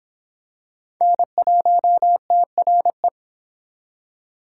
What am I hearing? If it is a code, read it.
Morse code, heard as a keyed tone, about 25 wpm, 700 Hz. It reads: N1TRE